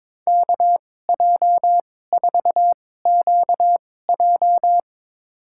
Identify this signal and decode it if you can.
Morse code, heard as a keyed tone, about 22 wpm, spaced as for 16 wpm, 700 Hz. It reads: KJ4QJ
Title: USA Call Signs - 5 Characters - 2x Character Spacing - 22wpm